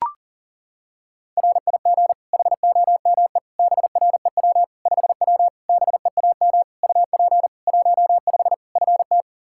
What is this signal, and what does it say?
Morse code, heard as a keyed tone, about 40 wpm, 700 Hz. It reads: RIG HOME BREW 5W BEAM UP 15 FT
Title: K1USN SST QSO Practice 40wpm